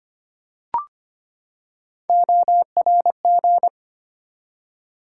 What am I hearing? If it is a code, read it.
Morse code, heard as a keyed tone, about 25 wpm, 700 Hz. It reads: ORG
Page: morse-code-ninja-practice-25wpm